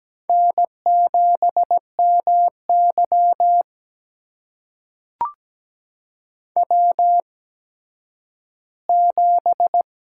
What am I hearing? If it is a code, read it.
Morse code, heard as a keyed tone, about 17 wpm, spaced as for 10 wpm, 700 Hz. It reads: N7MY W7
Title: USA Call Signs - 8x Character Spacing - 17wpm